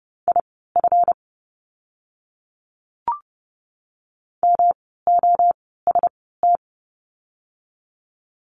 Morse code, heard as a keyed tone, about 30 wpm, spaced as for 17 wpm, 700 Hz. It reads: IF MOST
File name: course / 30:10wpm / 237 Lesson 38f - Letter q - Words 30wpm